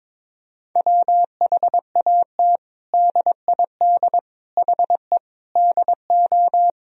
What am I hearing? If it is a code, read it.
Morse code, heard as a keyed tone, about 22 wpm, 700 Hz. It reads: WHAT DID HE DO
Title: Sentences from Top 100 Words and Abbreviations 22wpm